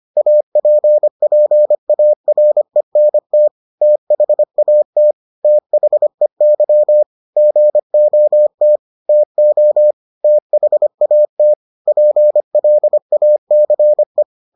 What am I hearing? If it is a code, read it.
Morse code, heard as a keyed tone, about 25 wpm, 600 Hz. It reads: APPARENT THAT THEY GOT TO THAT PLACE